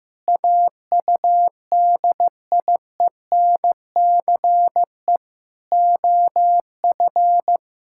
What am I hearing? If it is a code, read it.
Morse code, heard as a keyed tone, about 15 wpm, 700 Hz. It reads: AUDIENCE OF
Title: Sentences from Top 1000 Words - Review Part 5 - 15wpm